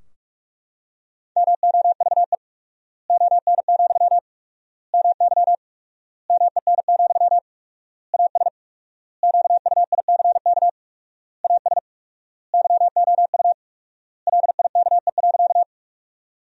Morse code, heard as a keyed tone, about 42 wpm, 700 Hz. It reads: MOVE ON, MY MEN, AS QUICK AS YOU LIKE.
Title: 4x Word-Spacing - Sentences from Top 500 Words - 45wpm